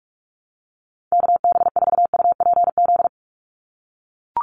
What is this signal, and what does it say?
Morse code, heard as a keyed tone, about 44 wpm, 700 Hz. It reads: KB4UPZ
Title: USA Call Signs 45wpm